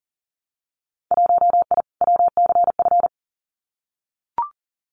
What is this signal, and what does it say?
Morse code, heard as a keyed tone, about 40 wpm, 700 Hz. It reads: 1I WCF
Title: Winter Field Day Exchanges Encore 40wpm